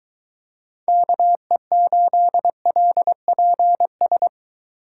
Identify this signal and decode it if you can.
Morse code, heard as a keyed tone, about 23 wpm, 700 Hz. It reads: KE8LPS